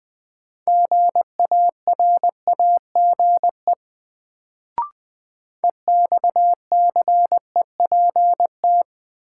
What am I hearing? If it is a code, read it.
Morse code, heard as a keyed tone, about 20 wpm, 700 Hz. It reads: GARAGE EXCEPT
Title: Sentences from Top 1600 Words - New Word Drill - 20wpm